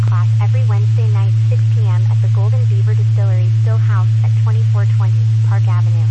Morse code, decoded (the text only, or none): none